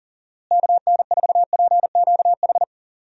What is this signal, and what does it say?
Morse code, heard as a keyed tone, about 37 wpm, 700 Hz. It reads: KN4PQH
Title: USA Call Signs - 6 Characters - Repeated 2x - 40wpm